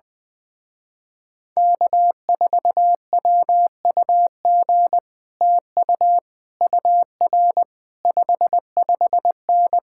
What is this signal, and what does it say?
Morse code, heard as a keyed tone, about 20 wpm, 700 Hz. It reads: K4WUG TU UR 55N
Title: Speed-Racing Common QSO Phrases Encore - 20wpm